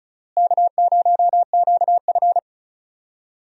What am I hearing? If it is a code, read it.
Morse code, heard as a keyed tone, about 34 wpm, 700 Hz. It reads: K0QF
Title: USA Call Signs - 4 Characters - Repeated 2x - 35wpm